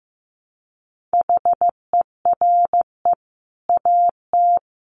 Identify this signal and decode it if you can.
Morse code, heard as a keyed tone, about 15 wpm, 700 Hz. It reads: HERE AT
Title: Sentences from Top 1100 Words 15wpm